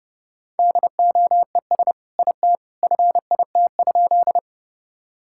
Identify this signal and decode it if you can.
Morse code, heard as a keyed tone, about 30 wpm, 700 Hz. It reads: DOES IT FIT?
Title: Sentences from Top 1000 Words - Review Part 5 - 30wpm